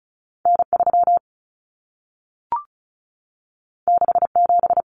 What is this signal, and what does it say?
Morse code, heard as a keyed tone, about 36 wpm, 700 Hz. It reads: N3 67
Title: Common Cut Numbers - 2-Digit 35wpm